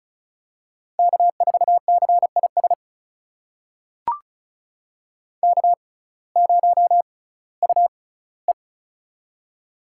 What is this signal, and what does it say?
Morse code, heard as a keyed tone, about 36 wpm, spaced as for 19 wpm, 700 Hz. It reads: K4CIS K0UE
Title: USA Call Signs - 6x Character Spacing - 35wpm